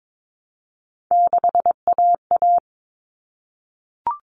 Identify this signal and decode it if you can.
Morse code, heard as a keyed tone, about 22 wpm, 700 Hz. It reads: 6AA
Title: Common Cut Numbers - 3-Digit 22wpm